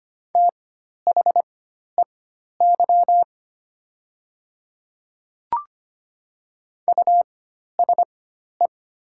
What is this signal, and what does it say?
Morse code, heard as a keyed tone, about 25 wpm, spaced as for 12 wpm, 690 Hz. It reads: THEY USE
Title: 4x Character Spacing - Top 200 Words - 25wpm